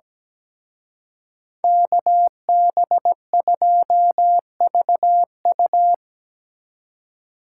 Morse code, heard as a keyed tone, about 19 wpm, 700 Hz. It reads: KB2VU
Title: Speed Racing - USA 2x2 Call Signs 17wpm